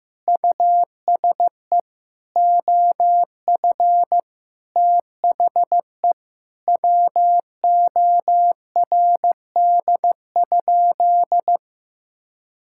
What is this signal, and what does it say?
Morse code, heard as a keyed tone, about 14 wpm, 700 Hz. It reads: USE OF THE WORD?
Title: Sentences from Top 1000 Words - Review Part 6 - 15wpm